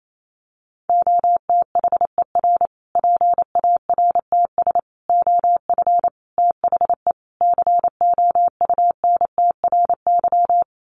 Morse code, heard as a keyed tone, about 28 wpm, 700 Hz. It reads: OTHER PARTS OF THE COUNTRY
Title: Sets of 5 Words Encore 28wpm